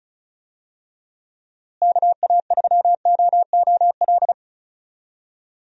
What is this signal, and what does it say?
Morse code, heard as a keyed tone, about 36 wpm, 700 Hz. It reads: KA3OOL
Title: USA Call Signs - 6 Characters - Repeated 3x - 35wpm